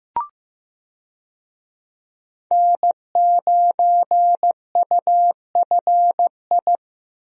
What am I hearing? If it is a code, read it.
Morse code, heard as a keyed tone, about 15 wpm, 700 Hz. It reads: N9UFI
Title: USA Call Signs - 5 Characters - Repeated 3x - 15wpm